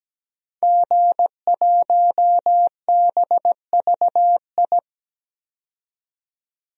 Morse code, heard as a keyed tone, about 17 wpm, 700 Hz. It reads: G1BVI
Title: DXCC-100 Call Signs - 5 Characters - 17wpm